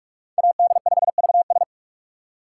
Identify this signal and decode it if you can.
Morse code, heard as a keyed tone, about 48 wpm, 700 Hz. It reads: AD5VS
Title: USA Call Signs - 5 Characters - 2x Character Spacing - 45wpm